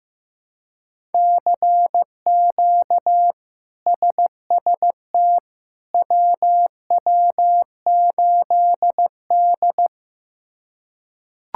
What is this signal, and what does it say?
Morse code, heard as a keyed tone, about 15 wpm, 700 Hz. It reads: CQ SST WW8D
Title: K1USN SST QSO Practice - 15wpm